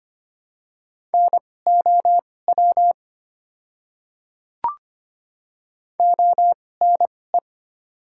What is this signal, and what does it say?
Morse code, heard as a keyed tone, about 25 wpm, spaced as for 18 wpm, 690 Hz. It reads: NOW ONE
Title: 2x Character Spacing - Top 100 Words - 25wpm